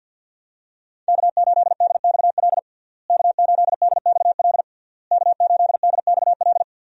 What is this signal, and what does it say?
Morse code, heard as a keyed tone, about 50 wpm, 700 Hz. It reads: K8DXL K8DXL K8DXL
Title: USA Call Signs - 5 Characters - Repeated 3x - 50wpm